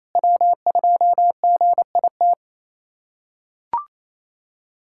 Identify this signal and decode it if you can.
Morse code, heard as a keyed tone, about 28 wpm, 700 Hz. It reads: W2GIT
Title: USA Callsign DE Callsign - 5 Characters - 28wpm